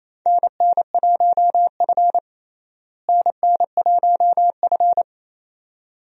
Morse code, heard as a keyed tone, about 28 wpm, 700 Hz. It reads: NN1F NN1F
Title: USA Call Signs - 4 Characters - Repeated 2x - 28wpm